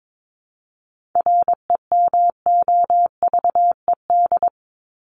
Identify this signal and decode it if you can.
Morse code, heard as a keyed tone, about 22 wpm, 700 Hz. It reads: REMOVED
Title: ED Suffix - Top 100 Words with Base Words 22wpm